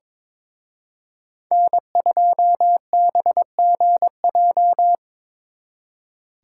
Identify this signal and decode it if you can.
Morse code, heard as a keyed tone, about 25 wpm, 700 Hz. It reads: N2BGJ
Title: Speed Racing - USA 1x3 Call Signs 22wpm